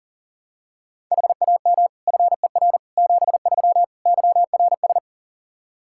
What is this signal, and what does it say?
Morse code, heard as a keyed tone, about 40 wpm, 700 Hz. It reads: HAM FER 73 YRS
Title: Practice QSO - Exchange 3 - More Info - 40wpm